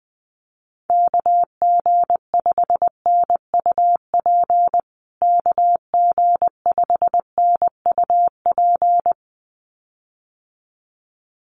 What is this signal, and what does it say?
Morse code, heard as a keyed tone, about 20 wpm, 700 Hz. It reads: KG5NUP KG5NUP
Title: USA Call Signs Repeated 2x 20wpm